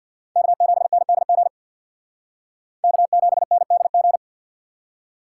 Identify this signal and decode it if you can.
Morse code, heard as a keyed tone, about 55 wpm, 700 Hz. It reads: K7NDG K7NDG
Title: Speed Racing - USA 1x3 Call Signs 50wpm